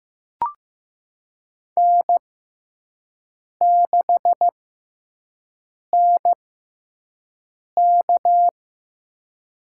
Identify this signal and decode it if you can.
Morse code, heard as a keyed tone, about 15 wpm, spaced as for 5 wpm, 700 Hz. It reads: N6NK
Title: USA Call Signs - 4 Characters - 6x Character Spacing - 15wpm